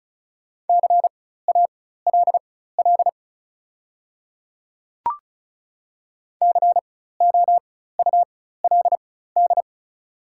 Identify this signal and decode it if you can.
Morse code, heard as a keyed tone, about 34 wpm, spaced as for 16 wpm, 700 Hz. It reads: CALL COULD
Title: 4x Character Spacing - Top 100 Words - 35wpm